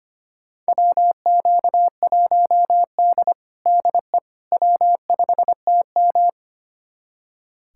Morse code, heard as a keyed tone, about 25 wpm, 700 Hz. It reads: WQ1D DE W5TM
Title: USA Callsign DE Callsign - 4 Characters - 25wpm